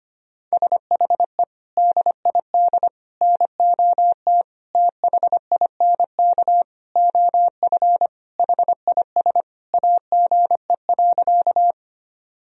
Morse code, heard as a keyed tone, about 25 wpm, 700 Hz. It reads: SHE DID NOT THINK OF HIS AGE.
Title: Sentences from Top 1000 Words - Review Part 3 - 25wpm